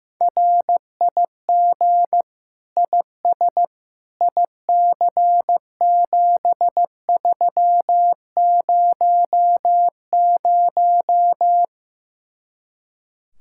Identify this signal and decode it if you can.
Morse code, heard as a keyed tone, about 15 wpm, 700 Hz. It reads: RIG IS IC7300
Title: Speed-Racing Common QSO Phrases Encore - 15wpm